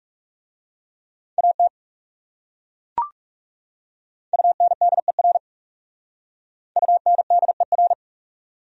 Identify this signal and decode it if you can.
Morse code, heard as a keyed tone, about 36 wpm, 700 Hz. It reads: AT UNDER UNDER
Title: Speed-Racing Top 200 Words 30wpm